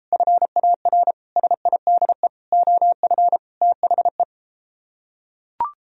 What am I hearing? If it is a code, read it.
Morse code, heard as a keyed tone, about 33 wpm, 700 Hz. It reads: FAR SIDE OF THE